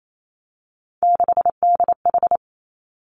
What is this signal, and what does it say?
Morse code, heard as a keyed tone, about 28 wpm, 700 Hz. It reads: 6DH
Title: Three Challenging Letter-Number Combinations 28wpm